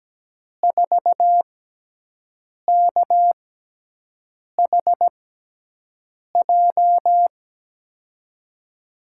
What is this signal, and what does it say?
Morse code, heard as a keyed tone, about 17 wpm, spaced as for 6 wpm, 700 Hz. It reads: 4KHJ
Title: USA Call Signs - 6x Character Spacing - 17wpm